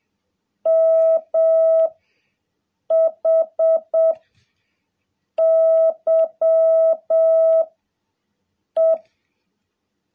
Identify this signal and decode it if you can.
Morse code, heard as a keyed tone, about 7 wpm, 660 Hz. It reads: M H Y E
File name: jh8cbh / 2021cw